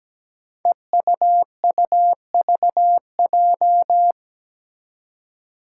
Morse code, heard as a keyed tone, about 17 wpm, 700 Hz. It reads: EUUVJ
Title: Five Challenging Letter-Number Combinations 17wpm